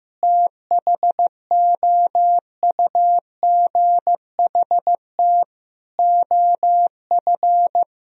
Morse code, heard as a keyed tone, about 15 wpm, 700 Hz. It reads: THOUGHT OF
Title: Sentences from Top 1000 Words - Review Part 1 - 15wpm